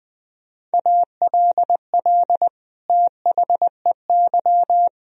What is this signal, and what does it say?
Morse code, heard as a keyed tone, about 20 wpm, 700 Hz. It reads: ALL THEY
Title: Sentences from Top 1000 Words - Review Part 1 - 20wpm